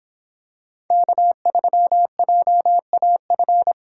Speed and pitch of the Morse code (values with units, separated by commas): 24 wpm, 700 Hz